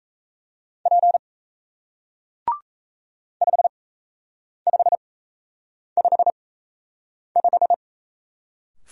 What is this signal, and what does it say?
Morse code, heard as a keyed tone, about 38 wpm, spaced as for 8 wpm, 700 Hz. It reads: P 5555